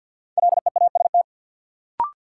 Morse code, heard as a keyed tone, about 53 wpm, 700 Hz. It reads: LEAST
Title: Sets of 2 Words - New Word Drill - 50wpm